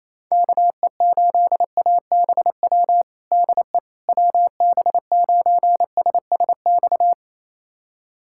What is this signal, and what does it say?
Morse code, heard as a keyed tone, about 28 wpm, 700 Hz. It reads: KE8ABW DE WB9SSX
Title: USA Callsign DE Callsign - 6 Characters - 28wpm